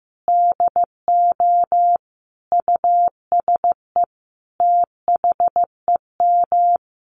Morse code, heard as a keyed tone, about 15 wpm, 700 Hz. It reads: DO USE THEM
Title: Sentences from Top 1300 Words 15wpm